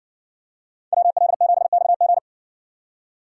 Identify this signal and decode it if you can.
Morse code, heard as a keyed tone, about 60 wpm, 700 Hz. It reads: WL7XG